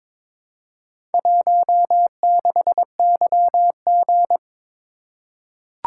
Morse code, heard as a keyed tone, about 23 wpm, 700 Hz. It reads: 16YG
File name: practice / 22wpm / Four Letter-Number Combinations - 22wpm